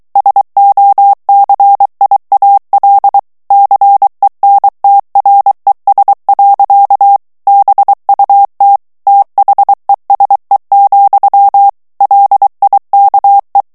Morse code, heard as a keyed tone, about 23 wpm, 800 Hz. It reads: SOCIAL CENTRES. BUT THESE, LIKE
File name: morse mp3 / ANGLAIS texte clair 2 vitesse 22 mots minute